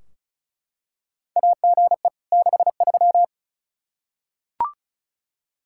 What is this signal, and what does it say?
Morse code, heard as a keyed tone, about 34 wpm, 700 Hz. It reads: AGE 63
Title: Practice QSO - Exchange 3 - More Info - 35wpm